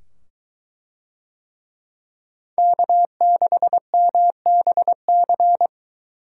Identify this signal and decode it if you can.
Morse code, heard as a keyed tone, about 22 wpm, 700 Hz. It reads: K6MBC